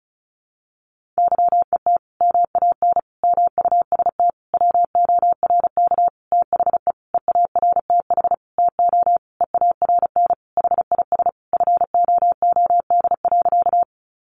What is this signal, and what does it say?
Morse code, heard as a keyed tone, about 34 wpm, 700 Hz. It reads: YET MAN MUST WORK THE EARTH TO EARN HIS FOOD.
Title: Sentences from Top 1500 Words 35wpm